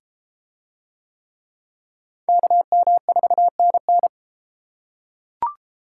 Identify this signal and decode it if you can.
Morse code, heard as a keyed tone, about 33 wpm, 700 Hz. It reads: KM4NN